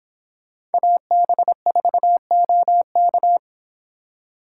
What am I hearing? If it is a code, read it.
Morse code, heard as a keyed tone, about 29 wpm, 700 Hz. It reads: AB4OK